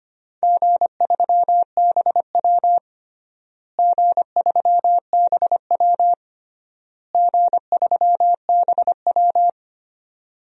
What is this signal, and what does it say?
Morse code, heard as a keyed tone, about 25 wpm, 700 Hz. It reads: G3BW G3BW G3BW
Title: DXCC-100 Call Signs - 4 Characters - Repeated 3x - 25wpm